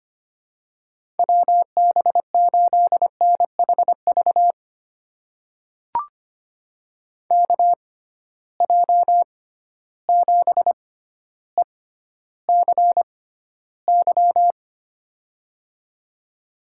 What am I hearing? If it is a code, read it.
Morse code, heard as a keyed tone, about 25 wpm, spaced as for 13 wpm, 700 Hz. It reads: WB8NHV KJ7ECY
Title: USA Call Signs - 6 Characters - 6x Character Spacing - 25wpm